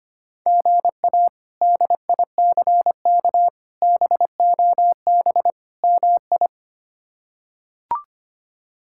Morse code, heard as a keyed tone, about 25 wpm, 700 Hz. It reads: GA DICK BOB MI
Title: K1USN SST QSO Practice - 25wpm